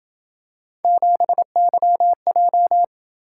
Morse code, heard as a keyed tone, about 28 wpm, 700 Hz. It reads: 7YJ